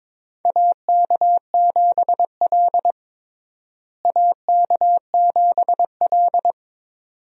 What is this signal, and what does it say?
Morse code, heard as a keyed tone, about 23 wpm, 700 Hz. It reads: AK7L AK7L
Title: USA Call Signs - 4 Characters - Repeated 3x - 22wpm